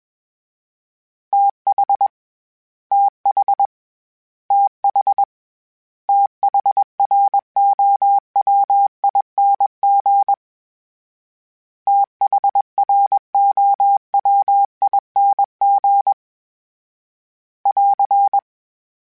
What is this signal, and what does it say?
Morse code, heard as a keyed tone, about 21 wpm, 800 Hz. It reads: TH TH TH THROWING THROWING <AR>
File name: THs / TH / TH Prefixes 3 @20